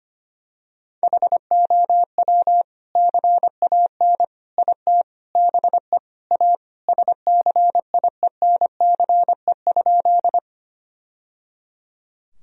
Morse code, heard as a keyed tone, about 25 wpm, 700 Hz. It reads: HOW CAN IT BE A SCIENCE?
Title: Sentences from Top 1000 Words - Review Part 2 - 25wpm